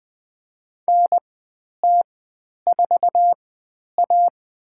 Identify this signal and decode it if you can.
Morse code, heard as a keyed tone, about 21 wpm, spaced as for 10 wpm, 700 Hz. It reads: NT4A
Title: 102 Lesson 17g - Letter u - Call Signs 20wpm